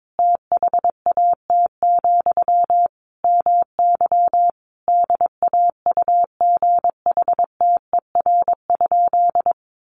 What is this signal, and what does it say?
Morse code, heard as a keyed tone, about 22 wpm, 700 Hz. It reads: THAT, MY DAUGHTER?
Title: Sentences from Top 1500 Words 22wpm